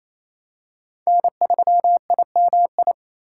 Morse code, heard as a keyed tone, about 28 wpm, 700 Hz. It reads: N3IMI
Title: USA Call Signs - 5 Characters - 8x Character Spacing - 28wpm